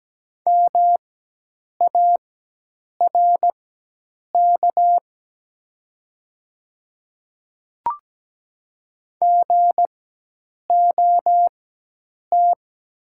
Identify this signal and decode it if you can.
Morse code, heard as a keyed tone, about 17 wpm, spaced as for 8 wpm, 700 Hz. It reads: MARK GOT
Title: 4x Character Spacing - Top 300 Words - 17wpm